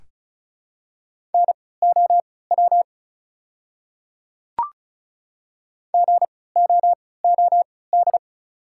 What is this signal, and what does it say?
Morse code, heard as a keyed tone, about 36 wpm, spaced as for 20 wpm, 690 Hz. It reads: NOW GOOD
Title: 3x Character Spacing - Top 300 Words - 35wpm